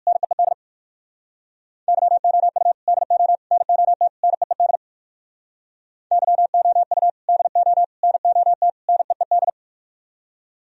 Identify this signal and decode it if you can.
Morse code, heard as a keyed tone, about 50 wpm, 700 Hz. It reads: NEED YOU DO NOT NEED YOU DO NOT NEED
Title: Speed-Racing - Sets of 4 Words 45wpm